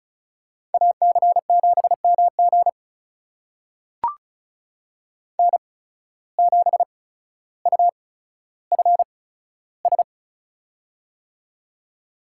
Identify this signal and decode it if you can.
Morse code, heard as a keyed tone, about 36 wpm, spaced as for 15 wpm, 700 Hz. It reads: AC7MG N7UFS
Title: USA Call Signs - 5 Characters - 8x Character Spacing - 35wpm